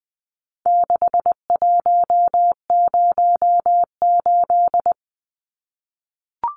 Morse code, heard as a keyed tone, about 20 wpm, 700 Hz. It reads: 6108
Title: Numbers - 4-Digit 20wpm